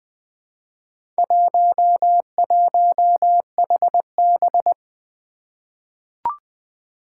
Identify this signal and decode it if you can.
Morse code, heard as a keyed tone, about 20 wpm, 700 Hz. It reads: 11HB
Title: Four Challenging Letter-Number Combinations 20wpm